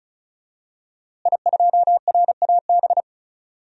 Morse code, heard as a keyed tone, about 34 wpm, 700 Hz. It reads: I2RAB
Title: DXCC-100 Call Signs - 5 Characters - Repeated 3x - 35wpm